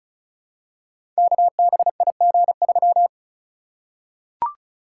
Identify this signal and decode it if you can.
Morse code, heard as a keyed tone, about 36 wpm, 690 Hz. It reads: KBIG3
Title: Code Groups - 5 Letter:Numbers - 35wpm